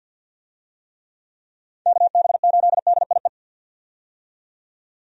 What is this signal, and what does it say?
Morse code, heard as a keyed tone, about 48 wpm, 700 Hz. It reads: KB8DIE
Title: USA Call Signs - 6 Characters - 50wpm